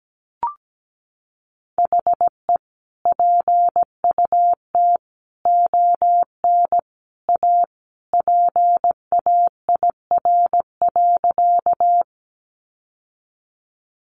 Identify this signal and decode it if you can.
Morse code, heard as a keyed tone, about 17 wpm, 700 Hz. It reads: HE PUT ON A PAIR.
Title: Sentences from Top 1200 Words 17wpm